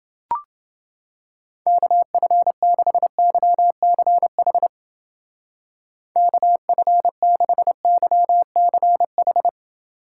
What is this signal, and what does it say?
Morse code, heard as a keyed tone, about 29 wpm, 700 Hz. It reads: KF6YCH KF6YCH